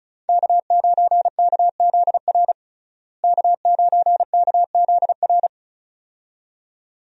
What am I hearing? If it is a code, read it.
Morse code, heard as a keyed tone, about 36 wpm, 700 Hz. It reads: K9KZR K9KZR
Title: USA Call Signs - 5 Characters - Repeated 2x - 35wpm